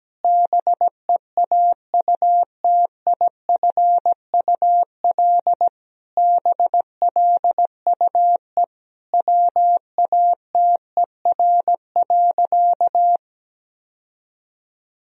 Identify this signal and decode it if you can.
Morse code, heard as a keyed tone, about 17 wpm, 700 Hz. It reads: BEAUTIFUL BLUE WATER.
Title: Sentences from Top 1000 Words - Review Part 3 - 17wpm